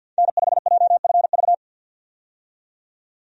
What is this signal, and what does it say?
Morse code, heard as a keyed tone, about 50 wpm, 700 Hz. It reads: N5JFV
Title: USA Call Signs - 5 Characters - 50wpm